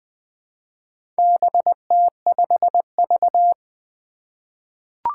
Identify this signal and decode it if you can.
Morse code, heard as a keyed tone, about 20 wpm, 700 Hz. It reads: BT5V
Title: Four Challenging Letter-Number Combinations 20wpm